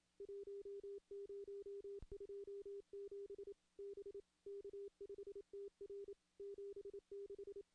Morse code, heard as a keyed tone, about 27 wpm, 400 Hz. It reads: 1027 B K5TR 76
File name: contests / CW-SS-k5tr- -1600